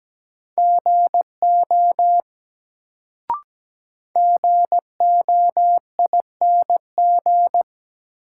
Morse code, heard as a keyed tone, about 17 wpm, 700 Hz. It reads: GO GOING
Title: Sets of 2 Words - New Word Drill - 17wpm